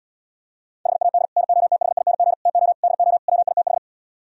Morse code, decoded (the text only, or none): HAD NOTHING TO DO WITH